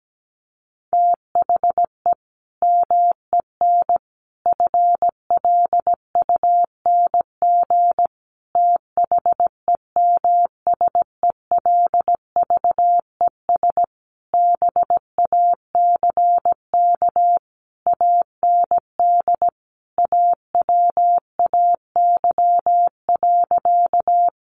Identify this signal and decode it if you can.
Morse code, heard as a keyed tone, about 17 wpm, 700 Hz. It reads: THE MEN FLUNG THEMSELVES BACK AND AWAY.
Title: A Sound Of Thunder 17wpm